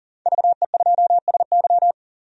Morse code, heard as a keyed tone, about 40 wpm, 700 Hz. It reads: VE2SY
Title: DXCC-100 Call Signs - 5 Characters - 40wpm